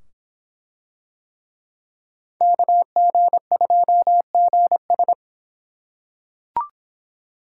Morse code, heard as a keyed tone, about 25 wpm, 700 Hz. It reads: KG2GS